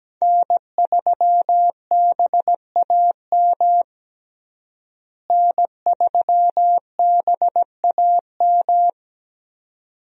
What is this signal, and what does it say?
Morse code, heard as a keyed tone, about 17 wpm, 700 Hz. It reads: N3BAM N3BAM
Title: USA Call Signs - 5 Characters - Repeated 2x - 17wpm